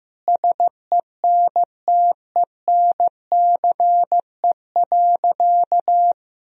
Morse code, heard as a keyed tone, about 15 wpm, 700 Hz. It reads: SENTENCE.
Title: Sentences from Top 1000 Words - Review Part 1 - 15wpm